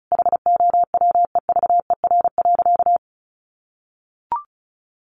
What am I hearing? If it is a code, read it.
Morse code, heard as a keyed tone, about 34 wpm, 700 Hz. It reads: HOWEVER.
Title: Sentences from Top 800 Words 35wpm